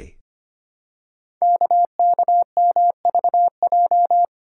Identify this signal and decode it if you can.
Morse code, heard as a keyed tone, about 25 wpm, 700 Hz. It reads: KKMVJ
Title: Code Groups - 5 Letters - 25wpm